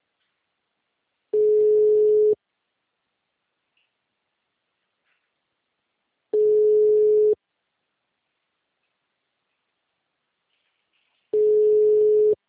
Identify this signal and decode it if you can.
Ringback tone (Europe, 425 Hz).
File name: out- -2010